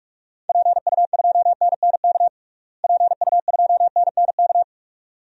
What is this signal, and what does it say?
Morse code, heard as a keyed tone, about 42 wpm, 700 Hz. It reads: PU2NNK PU2NNK
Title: DXCC-100 Call Signs - 6 Characters - Repeated 2x - 45wpm